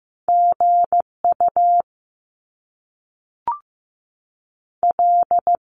Morse code, heard as a keyed tone, about 15 wpm, 700 Hz. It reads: G U L
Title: Top Two Letter Combinations in Words 15wpm